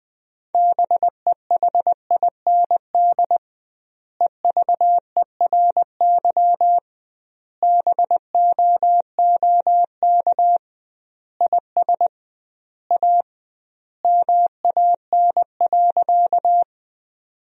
Morse code, heard as a keyed tone, about 21 wpm, 700 Hz. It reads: BEHIND EVERY BOOK IS A MAN.
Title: 2x Word-Spacing - Sentences from Top 500 Words - 20wpm